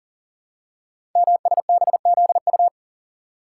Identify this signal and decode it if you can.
Morse code, heard as a keyed tone, about 43 wpm, 700 Hz. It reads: MSBZU
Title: Code Groups - 5 Letters - 40wpm